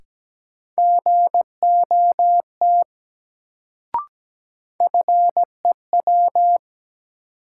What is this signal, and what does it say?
Morse code, heard as a keyed tone, about 17 wpm, 700 Hz. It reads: GOT FEW
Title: Sentences from Top 300 Words - New Word Drill - 17wpm